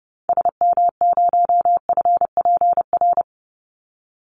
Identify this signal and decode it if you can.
Morse code, heard as a keyed tone, about 30 wpm, 700 Hz. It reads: SM0FPR
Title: Call Signs Repeated 2x - 30wpm